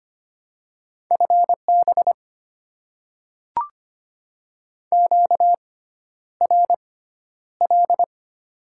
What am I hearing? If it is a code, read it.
Morse code, heard as a keyed tone, about 25 wpm, spaced as for 11 wpm, 690 Hz. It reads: FB QRL
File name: Top 20 QSO Elements - 6x Character Spacing - 25wpm